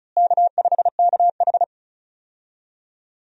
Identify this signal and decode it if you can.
Morse code, heard as a keyed tone, about 36 wpm, 700 Hz. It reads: K5KH
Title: USA Call Signs - 4 Characters - Repeated 2x - 35wpm